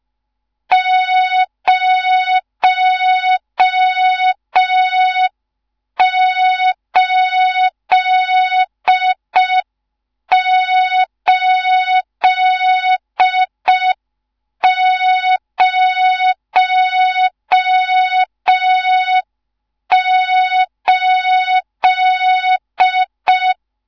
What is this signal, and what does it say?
Morse code, heard as a keyed tone, about 5 wpm, 3700 Hz, with noise. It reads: 08808
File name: Morze-HA2MN / hang / 14 280 HA2MN